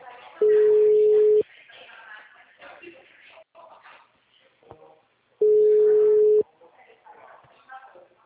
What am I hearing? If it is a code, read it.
Ringback tone (Europe, 425 Hz), with speech.
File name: out- -2008- - - .74085